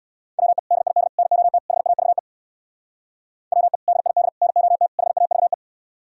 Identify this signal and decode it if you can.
Morse code, heard as a keyed tone, about 61 wpm, 700 Hz. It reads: WE DID NOT HAVE WE DID NOT HAVE
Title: Speed-Racing - Sets of 4 Words 50wpm